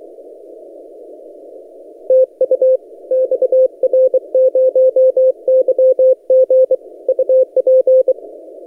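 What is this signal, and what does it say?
Morse code, heard as a keyed tone, about 24 wpm, 520 Hz. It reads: TU XR0YG UP
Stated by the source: XR0YG DX signal on 17 metres CW, posted by N6ML.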